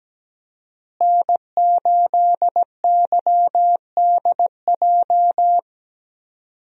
Morse code, heard as a keyed tone, about 17 wpm, 700 Hz. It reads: N8YDJ
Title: USA Call Signs - 5 Characters - Repeated 3x - 17wpm